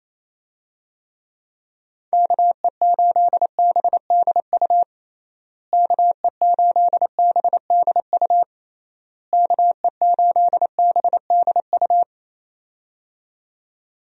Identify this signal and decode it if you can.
Morse code, heard as a keyed tone, about 28 wpm, 700 Hz. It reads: KE8BDU KE8BDU KE8BDU
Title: USA Call Signs - 6 Characters - Repeated 3x - 28wpm